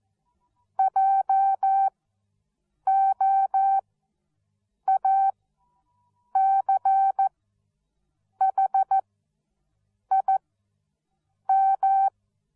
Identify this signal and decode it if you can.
Morse code, heard as a keyed tone, about 14 wpm, spaced as for 6 wpm, 780 Hz. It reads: JOACHIM
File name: Onlinelehrgang / morsen / DJ4UF Morsen13